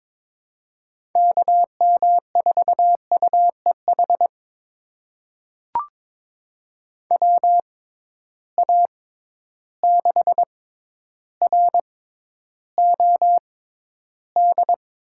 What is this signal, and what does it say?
Morse code, heard as a keyed tone, about 22 wpm, spaced as for 11 wpm, 700 Hz. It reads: KM4UEH WA6ROD
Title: USA Call Signs - 6 Characters - 6x Character Spacing - 22wpm